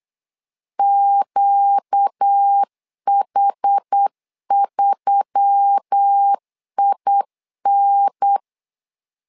Morse code, heard as a keyed tone, about 8 wpm, 800 Hz. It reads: QH3IN
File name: LEZIONI / 4 Q1S3 8wpm 3037